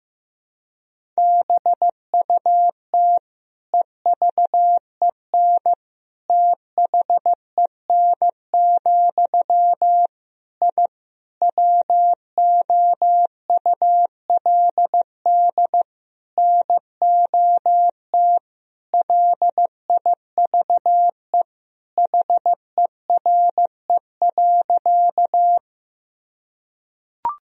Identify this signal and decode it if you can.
Morse code, heard as a keyed tone, about 15 wpm, 700 Hz. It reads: BUT EVEN THEN, I WOULD NOT LIVE HERE.
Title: Sentences from Top 1000 Words - Review Part 1 - 15wpm